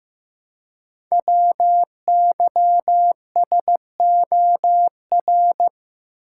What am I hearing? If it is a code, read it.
Morse code, heard as a keyed tone, about 15 wpm, 700 Hz. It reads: WYSOR
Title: Code Groups - 5 Letters - 15wpm